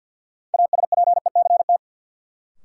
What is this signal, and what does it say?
Morse code, heard as a keyed tone, about 52 wpm, 700 Hz. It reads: ASPECT
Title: Sentences from Top 1300 Words - New Word Drill - 50wpm